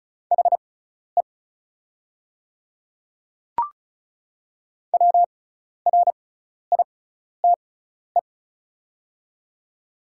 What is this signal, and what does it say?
Morse code, heard as a keyed tone, about 34 wpm, spaced as for 12 wpm, 690 Hz. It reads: HE WRITE
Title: 6x Character Spacing - Top 200 Words - 35wpm